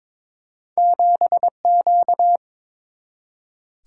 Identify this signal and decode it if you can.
Morse code, heard as a keyed tone, about 23 wpm, 700 Hz. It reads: 7Q
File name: Two Letter-Number Combinations - 22wpm